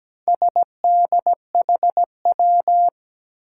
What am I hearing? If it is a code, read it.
Morse code, heard as a keyed tone, about 17 wpm, 700 Hz. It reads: SDHW
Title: Four Challenging Letter-Number Combinations 17wpm